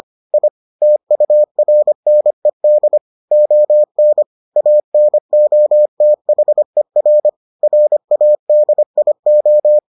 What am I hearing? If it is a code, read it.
Morse code, heard as a keyed tone, about 25 wpm, 600 Hz. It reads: I TURNED ON ANOTHER RADIO